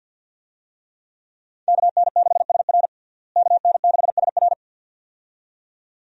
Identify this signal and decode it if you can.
Morse code, heard as a keyed tone, about 46 wpm, 700 Hz. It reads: KN6SR KN6SR
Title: USA Call Signs - 5 Characters - Repeated 2x - 50wpm